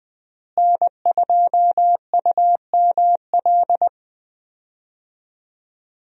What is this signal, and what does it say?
Morse code, heard as a keyed tone, about 20 wpm, 700 Hz. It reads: N2UML
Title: USA Call Signs - 5 Characters - Repeated 3x - 20wpm